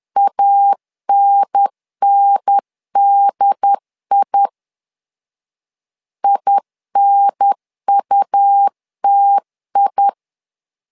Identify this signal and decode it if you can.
Morse code, heard as a keyed tone, about 10 wpm, 800 Hz. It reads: ANNDI INUTI
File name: LEZIONI / 2 UHDN / 2 UHDN 10wpm 1904